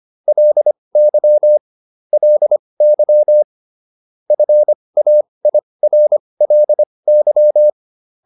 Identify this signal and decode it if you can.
Morse code, heard as a keyed tone, about 25 wpm, spaced as for 20 wpm, 600 Hz. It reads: LY LY FAIRLY
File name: LY6 20